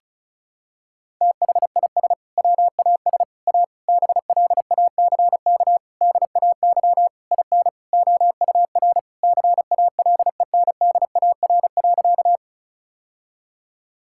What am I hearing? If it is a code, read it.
Morse code, heard as a keyed tone, about 36 wpm, 700 Hz. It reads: THIS WAS A BLACK DAY IN OUR CALENDAR.
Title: Sentences from Top 1600 Words 35wpm